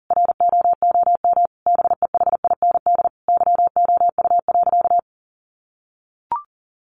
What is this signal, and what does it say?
Morse code, heard as a keyed tone, about 40 wpm, 700 Hz. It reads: ROOM BEHIND YOU.
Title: Sentences from Top 500 Words 40wpm